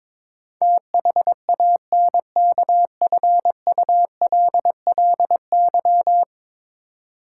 Morse code, heard as a keyed tone, about 22 wpm, 700 Hz. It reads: THANKFULLY
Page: morse-code-ninja-practice-22wpm